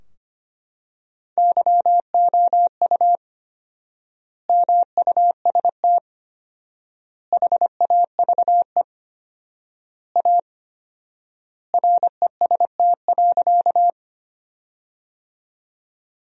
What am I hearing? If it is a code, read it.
Morse code, heard as a keyed tone, about 25 wpm, 700 Hz. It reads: YOU MUST HAVE A REST.
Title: 4x Word-Spacing - Sentences from Top 500 Words - 25wpm